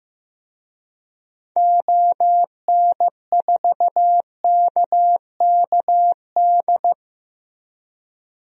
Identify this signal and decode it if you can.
Morse code, heard as a keyed tone, about 15 wpm, 700 Hz. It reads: ON4KKD
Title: DXCC-100 Call Signs - 6 Characters - Repeated 2x - 15wpm